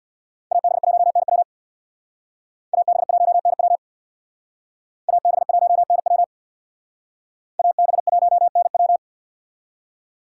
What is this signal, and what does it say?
Morse code, heard as a keyed tone, about 62 wpm, 700 Hz. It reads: AB1NW AB1NW AB1NW AB1NW
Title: Speed Racing - USA 2x2 Call Signs 50wpm